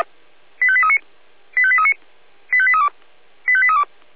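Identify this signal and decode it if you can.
Five-tone selective call (ZVEI), heard as quick runs of five tones, tones of about 80 ms. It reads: 85839 85839 85832 85832
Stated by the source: ZVEI five-tone selective call 85830 received via SDR.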